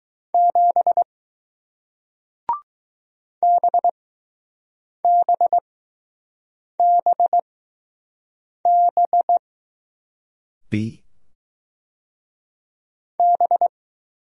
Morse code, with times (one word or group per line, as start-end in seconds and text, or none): 0.3-1.0 7
3.4-9.4 BBBB
13.2-13.7 B